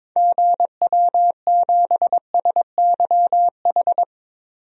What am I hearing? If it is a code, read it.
Morse code, heard as a keyed tone, about 22 wpm, 700 Hz. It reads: GW7SYH
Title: DXCC-100 Call Signs - 6 Characters - 22wpm